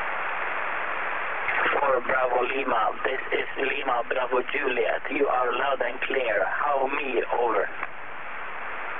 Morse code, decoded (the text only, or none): none